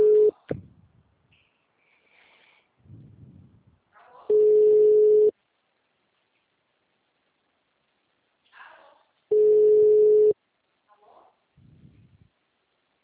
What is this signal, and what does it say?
Ringback tone (Europe, 425 Hz), with speech.
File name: out- -2001- - - .7038